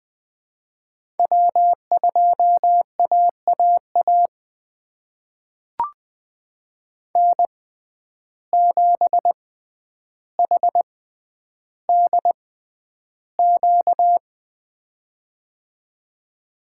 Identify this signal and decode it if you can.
Morse code, heard as a keyed tone, about 20 wpm, spaced as for 10 wpm, 700 Hz. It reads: W2AAA N7HDQ
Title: USA Call Signs - 5 Characters - 6x Character Spacing - 20wpm